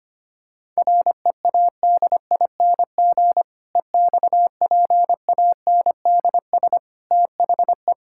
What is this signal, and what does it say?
Morse code, heard as a keyed tone, about 25 wpm, 700 Hz. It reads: READING EXPANDS THE
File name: practice / 25wpm / Difficult Sentences with Hints - 25wpm